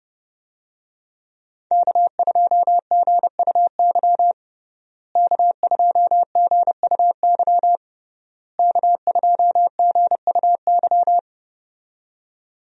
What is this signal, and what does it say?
Morse code, heard as a keyed tone, about 31 wpm, 700 Hz. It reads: K2GUY K2GUY K2GUY
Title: USA Call Signs - 5 Characters - Repeated 3x - 30wpm